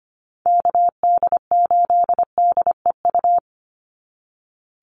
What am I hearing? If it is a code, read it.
Morse code, heard as a keyed tone, about 25 wpm, 700 Hz. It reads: KD8DEU
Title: NA SOTA Chaser Callsigns 25wpm